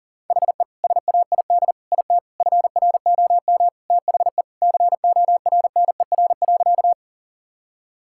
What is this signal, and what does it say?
Morse code, heard as a keyed tone, about 40 wpm, 700 Hz. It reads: HE SAID IT FROM THE CORNER.
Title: Sentences from Top 1600 Words 40wpm